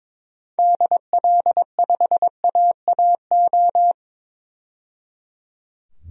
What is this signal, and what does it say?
Morse code, heard as a keyed tone, about 22 wpm, 700 Hz. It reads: DL5AAO
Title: DXCC-100 Call Signs - 6 Characters - 22wpm